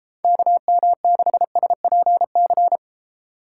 Morse code, heard as a keyed tone, about 32 wpm, 700 Hz. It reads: KM6SPC